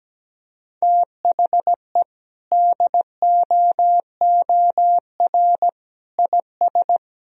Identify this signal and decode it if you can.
Morse code, heard as a keyed tone, about 17 wpm, 700 Hz. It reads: THE DOOR IS
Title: Sentences from Top 1000 Words - Review Part 1 - 17wpm